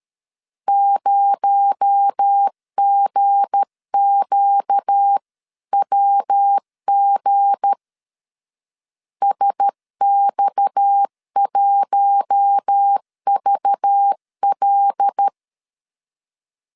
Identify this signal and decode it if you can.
Morse code, heard as a keyed tone, about 13 wpm, 800 Hz. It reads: 0GQWG SX1VL
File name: LEZIONI / 7 GF9Z / 7 G 12wpm 2055